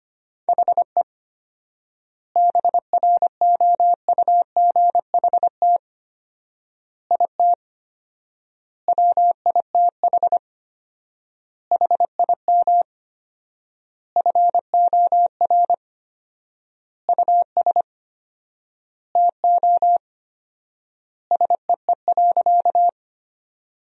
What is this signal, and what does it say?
Morse code, heard as a keyed tone, about 25 wpm, 700 Hz. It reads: HE BROUGHT IT WITH HIM FOR US TO SEE.
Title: 4x Word-Spacing - Sentences from Top 500 Words - 25wpm